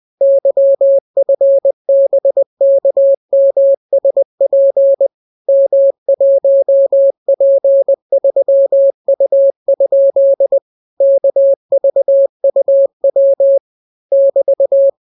Morse code, heard as a keyed tone, about 20 wpm, 550 Hz. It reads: YFBKMSP M1P3U? KVUW <BT>